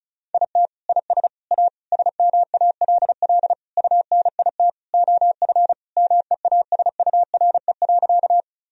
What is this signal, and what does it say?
Morse code, heard as a keyed tone, about 36 wpm, 700 Hz. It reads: IT IS A SMALL UNIT OF MEASURE.
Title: Sentences from Top 1000 Words - Review Part 2 - 35wpm